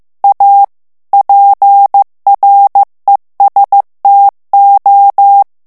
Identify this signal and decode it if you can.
Morse code, heard as a keyed tone, about 15 wpm, 800 Hz. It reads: A PRESTO
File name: ITALIEN texte clair 1 vitesse 14 mots minute